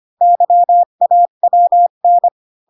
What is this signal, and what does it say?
Morse code, heard as a keyed tone, about 25 wpm, 700 Hz. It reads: YAWN
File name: Pinocchio / Pinocchio ch5 20 25 ik1hge